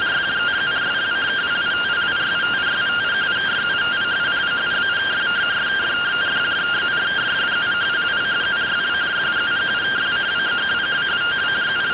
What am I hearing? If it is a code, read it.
Two-tone teleprinter signal, about 170 Hz shift.